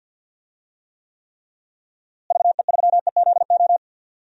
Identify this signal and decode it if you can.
Morse code, heard as a keyed tone, about 51 wpm, 700 Hz. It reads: VE3EZO
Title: DXCC-100 Call Signs - 6 Characters - Repeated 3x - 50wpm